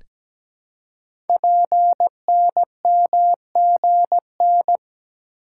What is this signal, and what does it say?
Morse code, heard as a keyed tone, about 17 wpm, 700 Hz. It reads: PNMGN
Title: Code Groups - 5 Letters - 17wpm